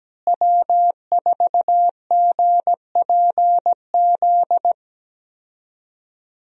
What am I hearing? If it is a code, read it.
Morse code, heard as a keyed tone, about 17 wpm, 700 Hz. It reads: W4GPZ
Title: USA Call Signs - 5 Characters - Repeated 3x - 17wpm